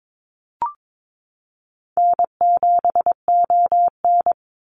Morse code, heard as a keyed tone, about 22 wpm, 700 Hz. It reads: N7ON
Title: Easy Call Signs - 22wpm